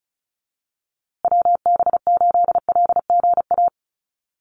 Morse code, heard as a keyed tone, about 34 wpm, 700 Hz. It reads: WB8LGA
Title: USA Call Signs Repeated 2x 35wpm